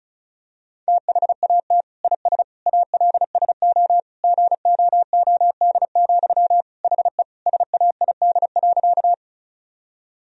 Morse code, heard as a keyed tone, about 36 wpm, 700 Hz. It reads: THAT IS ALSO GOOD, HE SAID.
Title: Sentences from Top 1000 Words - Review Part 1 - 35wpm